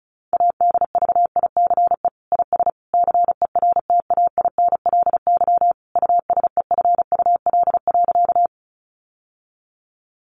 Morse code, heard as a keyed tone, about 34 wpm, 700 Hz. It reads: ADVICE IS CERTAINLY USEFUL.
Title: Sentences from Top 1100 Words 35wpm